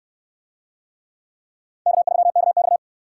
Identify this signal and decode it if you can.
Morse code, heard as a keyed tone, about 66 wpm, 700 Hz. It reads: K3KX